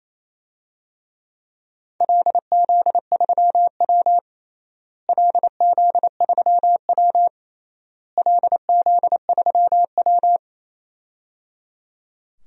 Morse code, heard as a keyed tone, about 28 wpm, 700 Hz. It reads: LZ3W LZ3W LZ3W
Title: DXCC-100 Call Signs - 4 Characters - Repeated 3x - 28wpm